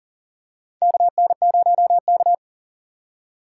Morse code, heard as a keyed tone, about 39 wpm, 700 Hz. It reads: KN0K
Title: USA Call Signs - 4 Characters - 2x Character Spacing - 40wpm